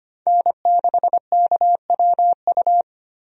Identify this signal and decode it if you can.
Morse code, heard as a keyed tone, about 25 wpm, 690 Hz. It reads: N6KWU
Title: USA Call Signs - 5 Characters - 25wpm